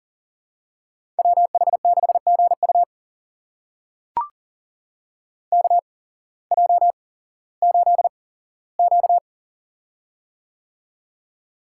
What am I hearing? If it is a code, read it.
Morse code, heard as a keyed tone, about 40 wpm, spaced as for 19 wpm, 700 Hz. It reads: WH6GU KJ8Q
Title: USA Call Signs - 8x Character Spacing - 40wpm